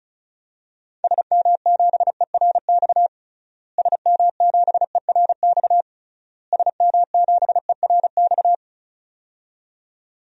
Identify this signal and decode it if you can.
Morse code, heard as a keyed tone, about 36 wpm, 700 Hz. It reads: SM7ERX SM7ERX SM7ERX
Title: DXCC-100 Call Signs - 6 Characters - Repeated 3x - 35wpm